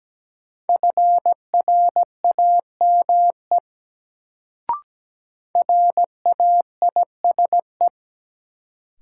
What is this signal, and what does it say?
Morse code, heard as a keyed tone, about 17 wpm, 700 Hz. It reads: FRAME RAISE
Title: Sentences from Top 1400 Words - New Word Drill Alt - 17wpm